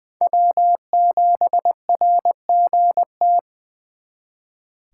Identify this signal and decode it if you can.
Morse code, heard as a keyed tone, about 24 wpm, 700 Hz. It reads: W7RGT
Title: Speed Racing - USA 1x3 Call Signs 20wpm